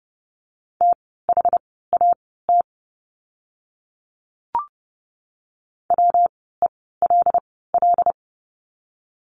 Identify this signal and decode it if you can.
Morse code, heard as a keyed tone, about 31 wpm, spaced as for 17 wpm, 700 Hz. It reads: THAT WELL
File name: course / 30:10wpm / 115 Lesson 19f - Letter w - Words 30wpm